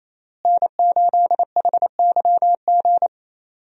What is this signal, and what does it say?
Morse code, heard as a keyed tone, about 28 wpm, 700 Hz. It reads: N8HYG
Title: USA Call Signs - 5 Characters - Repeated 2x - 28wpm